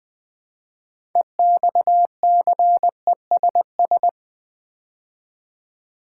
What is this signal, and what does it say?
Morse code, heard as a keyed tone, about 20 wpm, 700 Hz. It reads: EXCESS
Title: Build A Word - 2 Letters At A Time - 6-Letter Words - 20wpm